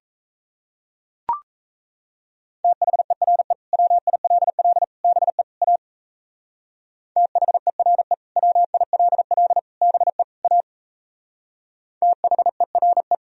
Morse code, heard as a keyed tone, about 35 wpm, 700 Hz. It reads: THERE WILL BE A THERE WILL BE A THERE